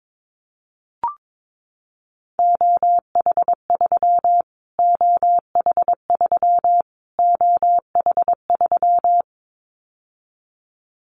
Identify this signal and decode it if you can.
Morse code, heard as a keyed tone, about 22 wpm, 700 Hz. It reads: OH3 OH3 OH3
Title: Callsign Prefixes for Top 25 DXCC - Repeated 3x - 22wpm